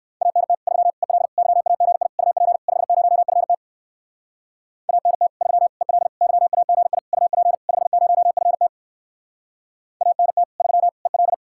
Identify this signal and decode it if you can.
Morse code, heard as a keyed tone, about 59 wpm, 700 Hz. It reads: ANT 3 EL YAGI UP 50FT ANT 3 EL YAGI UP 50FT ANT 3 EL